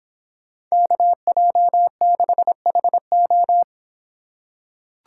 Morse code, heard as a keyed tone, about 25 wpm, 700 Hz. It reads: KJ6HO